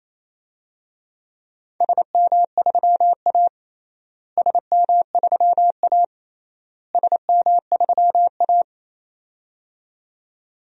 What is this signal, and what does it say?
Morse code, heard as a keyed tone, about 28 wpm, 700 Hz. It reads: SM3A SM3A SM3A
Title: DXCC-100 Call Signs - 4 Characters - Repeated 3x - 28wpm